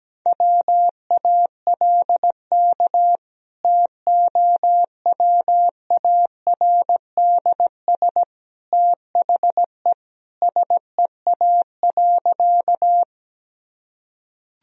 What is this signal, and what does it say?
Morse code, heard as a keyed tone, about 17 wpm, 700 Hz. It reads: WALK TOWARDS THE SEA.
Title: Sentences from Top 1000 Words - Review Part 2 - 17wpm